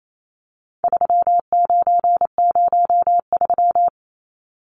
Morse code, heard as a keyed tone, about 28 wpm, 700 Hz. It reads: 3903
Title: Numbers - 4-Digit 28wpm